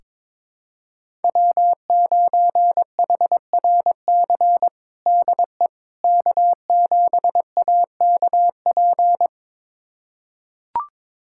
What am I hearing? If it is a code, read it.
Morse code, heard as a keyed tone, about 22 wpm, 700 Hz. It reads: W9HRC DE K7AKP
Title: USA Callsign DE Callsign - 5 Characters - 22wpm